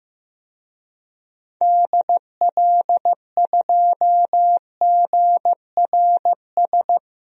Morse code, heard as a keyed tone, about 15 wpm, 700 Hz. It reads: DL2GRS
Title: DXCC-100 Call Signs - 6 Characters - Repeated 3x - 15wpm